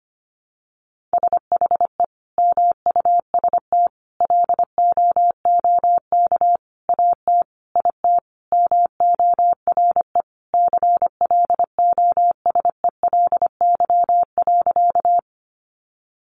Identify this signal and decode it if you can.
Morse code, heard as a keyed tone, about 25 wpm, 700 Hz. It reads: SHE MUST LOOK AT IT MORE CLOSELY.
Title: Sentences from Top 1400 Words 25wpm